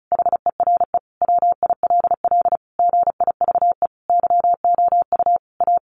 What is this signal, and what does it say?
Morse code, heard as a keyed tone, about 34 wpm, 700 Hz. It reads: HERE WILL GIVE YOU A
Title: Sentences from Top 1100 Words 35wpm